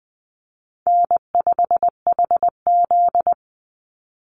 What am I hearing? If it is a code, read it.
Morse code, heard as a keyed tone, about 20 wpm, 700 Hz. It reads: N5HZ
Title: Call Signs - 20wpm